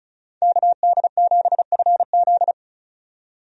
Morse code, heard as a keyed tone, about 34 wpm, 700 Hz. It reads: KD7FZ
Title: USA Call Signs - 5 Characters - 35wpm